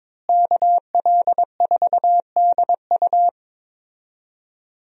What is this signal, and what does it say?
Morse code, heard as a keyed tone, about 22 wpm, 700 Hz. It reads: KL4DU
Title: USA Call Signs - 5 Characters - Repeated 3x - 22wpm